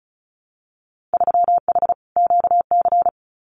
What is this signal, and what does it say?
Morse code, heard as a keyed tone, about 36 wpm, 700 Hz. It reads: 3H QC
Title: Winter Field Day Exchanges Encore 35wpm